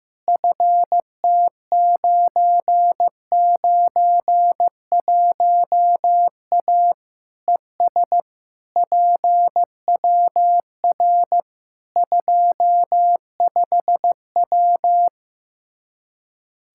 Morse code, heard as a keyed tone, about 15 wpm, 700 Hz. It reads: FT991A ES PWR 25W
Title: Practice QSO - Exchange 2 - Info - 15wpm